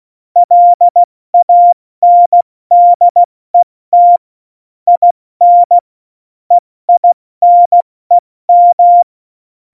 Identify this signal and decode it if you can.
Morse code, heard as a keyed tone, about 16 wpm, 700 Hz. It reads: LANDET IN EINEM
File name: Morsekurs / Witze / Witz 17. November 2020, 70 F80